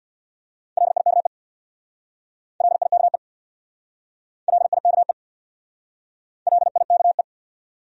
Morse code, heard as a keyed tone, about 67 wpm, 700 Hz. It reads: LIKE LIKE LIKE LIKE